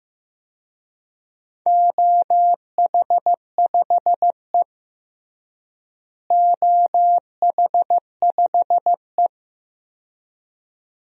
Morse code, heard as a keyed tone, about 16 wpm, 700 Hz. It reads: OH5E OH5E
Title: DXCC-100 Call Signs - 4 Characters - Repeated 2x - 15wpm